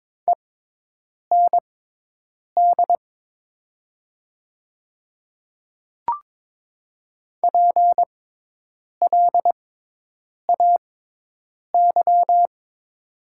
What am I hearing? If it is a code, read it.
Morse code, heard as a keyed tone, about 22 wpm, spaced as for 8 wpm, 700 Hz. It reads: END PLAY
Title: 6x Character Spacing - Top 200 Words - 22wpm